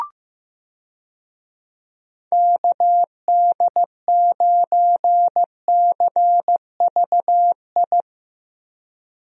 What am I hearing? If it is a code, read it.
Morse code, heard as a keyed tone, about 15 wpm, 700 Hz. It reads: KD9CVI
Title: USA Call Signs - 6 Characters - Repeated 3x - 15wpm